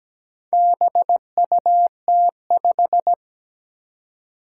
Morse code, heard as a keyed tone, about 17 wpm, 700 Hz. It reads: BUT5
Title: Four Challenging Letter-Number Combinations 17wpm